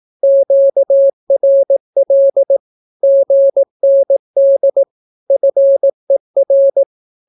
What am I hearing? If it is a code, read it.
Morse code, heard as a keyed tone, about 18 wpm, 550 Hz. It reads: QRL GND FER
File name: naukapodstaw / skroty / skroty